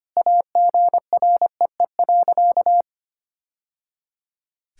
Morse code, heard as a keyed tone, about 25 wpm, 700 Hz. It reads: AGREE.
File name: practice / 25wpm / Sentences with 2-Word Phrases - 25wpm